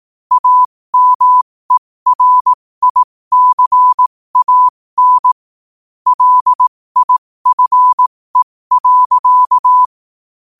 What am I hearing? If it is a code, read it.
Morse code, heard as a keyed tone, about 18 wpm, spaced as for 15 wpm, 1000 Hz. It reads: AMERICAN LIFE.
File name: qotdfiles / 15wpm